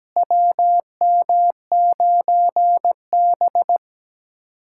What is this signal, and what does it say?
Morse code, heard as a keyed tone, about 17 wpm, 700 Hz. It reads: WM9B
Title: USA Callsign DE Callsign - 4 Characters - 17wpm